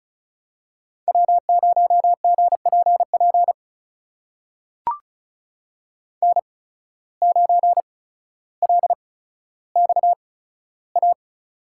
Morse code, heard as a keyed tone, about 36 wpm, spaced as for 15 wpm, 700 Hz. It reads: W0GPP N9LXA
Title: USA Call Signs - 5 Characters - 8x Character Spacing - 35wpm